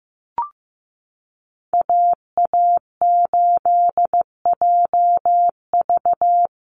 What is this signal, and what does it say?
Morse code, heard as a keyed tone, about 15 wpm, 700 Hz. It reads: AA8JV
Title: USA Call Signs Repeated 2x 15wpm